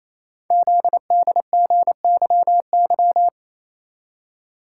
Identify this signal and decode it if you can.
Morse code, heard as a keyed tone, about 28 wpm, 700 Hz. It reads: ZDGYY
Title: Code Groups - 5 Letters - 28wpm